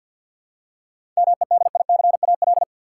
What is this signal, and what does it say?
Morse code, heard as a keyed tone, about 52 wpm, 700 Hz. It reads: MEDICAL
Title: Seven Letter Words - 50wpm